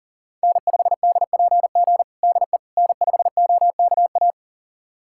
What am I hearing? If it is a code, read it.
Morse code, heard as a keyed tone, about 40 wpm, 700 Hz. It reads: N5DPG DE N5OKA
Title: USA Callsign DE Callsign - 5 Characters - 40wpm